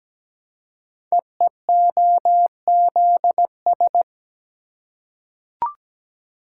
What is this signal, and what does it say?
Morse code, heard as a keyed tone, about 17 wpm, 700 Hz. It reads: EEOZS
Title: Code Groups - 5 Letters - 17wpm